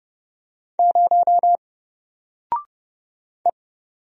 Morse code, heard as a keyed tone, about 30 wpm, 700 Hz. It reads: 0 E
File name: Single-Letter-Number-30wpm